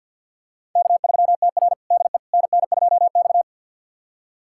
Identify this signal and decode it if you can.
Morse code, heard as a keyed tone, about 52 wpm, 700 Hz. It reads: K3TR DE NN2X
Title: USA Callsign DE Callsign - 4 Characters - 50wpm